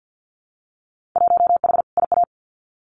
Morse code, heard as a keyed tone, about 50 wpm, 700 Hz. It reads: 1H IA
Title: Winter Field Day Exchanges 50wpm